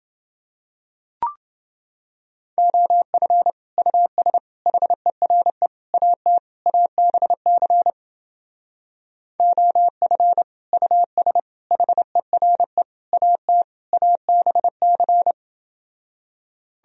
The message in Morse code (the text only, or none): OF US HERE AT ABC OF US HERE AT ABC